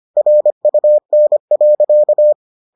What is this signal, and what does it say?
Morse code, heard as a keyed tone, about 25 wpm, 600 Hz. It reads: RUN.